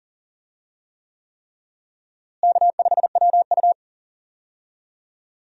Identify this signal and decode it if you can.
Morse code, heard as a keyed tone, about 40 wpm, 700 Hz. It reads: K5WU
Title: USA Call Signs - 4 Characters - Repeated 3x - 40wpm